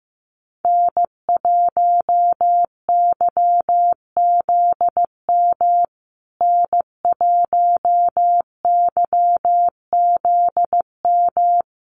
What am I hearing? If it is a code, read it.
Morse code, heard as a keyed tone, about 15 wpm, 700 Hz. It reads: N1YZM N1YZM
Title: USA Call Signs Repeated 3x 15wpm